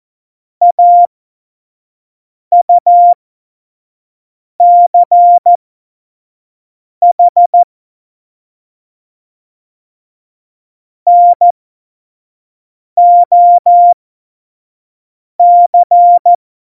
Morse code, heard as a keyed tone, about 14 wpm, spaced as for 5 wpm, 700 Hz. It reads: AUCH NOC
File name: Morsekurs / Witze / Witz 28 Juli 2020, 25 F70